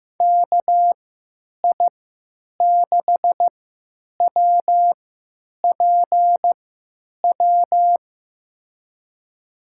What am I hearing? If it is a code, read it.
Morse code, heard as a keyed tone, about 15 wpm, spaced as for 9 wpm, 700 Hz. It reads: KI6WPW
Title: USA Call Signs - 6 Characters - 3x Character Spacing - 15wpm